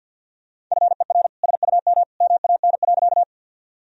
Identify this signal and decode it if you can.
Morse code, heard as a keyed tone, about 50 wpm, 700 Hz. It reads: FER SUM MAN.
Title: Sentences from Top 200 Words and Abbreviations 50wpm